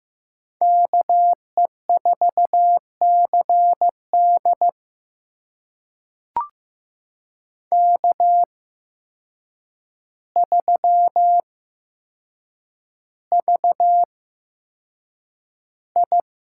Morse code, heard as a keyed tone, about 15 wpm, spaced as for 7 wpm, 700 Hz. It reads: KE4CD K3VI
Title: USA Call Signs - 8x Character Spacing - 15wpm